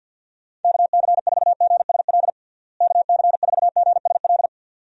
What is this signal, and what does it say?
Morse code, heard as a keyed tone, about 46 wpm, 700 Hz. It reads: KC4GSL KC4GSL
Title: USA Call Signs - 6 Characters - Repeated 2x - 50wpm